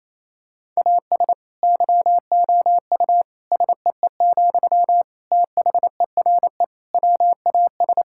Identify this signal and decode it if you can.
Morse code, heard as a keyed tone, about 28 wpm, 700 Hz. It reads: AS YOU SEE, THERE WAS
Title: Sentences from Top 1000 Words - Review Part 6 - 28wpm